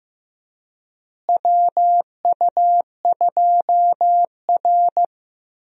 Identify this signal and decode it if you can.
Morse code, heard as a keyed tone, about 15 wpm, 700 Hz. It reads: WU2R
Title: USA Call Signs - 4 Characters - Repeated 2x - 15wpm